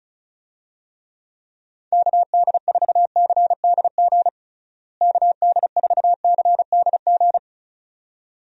Morse code, heard as a keyed tone, about 36 wpm, 700 Hz. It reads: KD4CDG KD4CDG
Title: USA Call Signs - 6 Characters - Repeated 2x - 35wpm